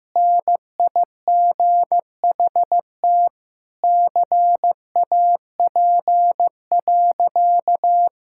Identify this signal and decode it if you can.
Morse code, heard as a keyed tone, about 16 wpm, 700 Hz. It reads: NIGHT CAP.
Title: Sentences from Top 1600 Words 15wpm